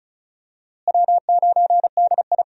Morse code, heard as a keyed tone, about 36 wpm, 700 Hz. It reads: W9DI
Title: USA Call Signs - 4 Characters - Repeated 3x - 35wpm